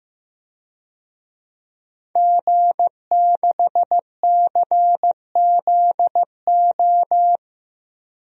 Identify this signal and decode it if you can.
Morse code, heard as a keyed tone, about 15 wpm, 700 Hz. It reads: G6CZO
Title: DXCC-100 Call Signs - 5 Characters - 15wpm